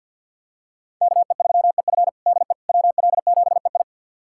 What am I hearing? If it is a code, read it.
Morse code, heard as a keyed tone, about 50 wpm, 700 Hz. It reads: KE3EF DE WL7EI
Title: USA Callsign DE Callsign - 5 Characters - 50wpm